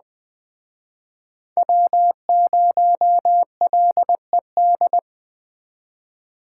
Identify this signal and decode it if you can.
Morse code, heard as a keyed tone, about 20 wpm, 700 Hz. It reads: W0LED
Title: Speed Racing - USA 1x3 Call Signs 17wpm